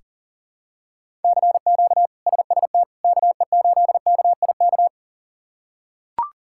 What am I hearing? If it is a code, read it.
Morse code, heard as a keyed tone, about 37 wpm, 700 Hz. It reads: CQ SST KE8KIK
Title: K1USN SST QSO Practice - 40wpm